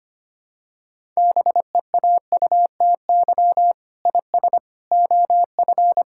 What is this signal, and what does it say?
Morse code, heard as a keyed tone, about 25 wpm, 700 Hz. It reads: BEAUTY IS OF
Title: Sentences from Top 1000 Words - Review Part 5 - 25wpm